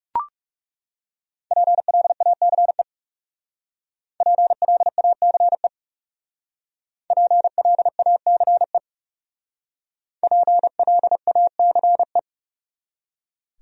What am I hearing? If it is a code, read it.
Morse code, heard as a keyed tone, about 36 wpm, 700 Hz. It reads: PLACE PLACE PLACE PLACE
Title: Speed-Racing Top 200 Words 30wpm